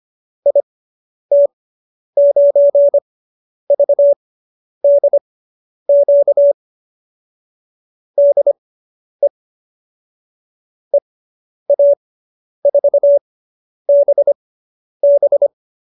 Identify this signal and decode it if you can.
Morse code, heard as a keyed tone, about 25 wpm, spaced as for 10 wpm, 570 Hz. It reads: IT9VDQ DE EA4BB